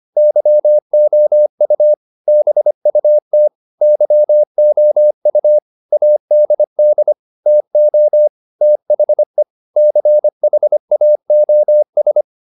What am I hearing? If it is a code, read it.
Morse code, heard as a keyed tone, about 25 wpm, 600 Hz. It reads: YOU BUT YOU ADD TO THE CHAOS